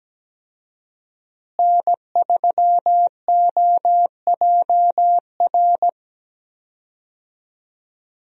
Morse code, heard as a keyed tone, about 17 wpm, 700 Hz. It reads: N3OJR
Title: USA Call Signs - 5 Characters - Repeated 2x - 17wpm